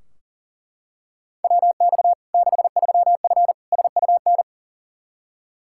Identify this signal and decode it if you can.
Morse code, heard as a keyed tone, about 40 wpm, 700 Hz. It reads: WX 63F SUN
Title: Practice QSO - Exchange 2 - Info - 40wpm